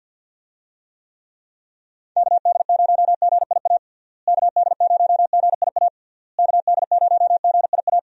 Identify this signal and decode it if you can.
Morse code, heard as a keyed tone, about 50 wpm, 700 Hz. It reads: KD0GIA KD0GIA KD0GIA
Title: USA Call Signs - 6 Characters - Repeated 3x - 50wpm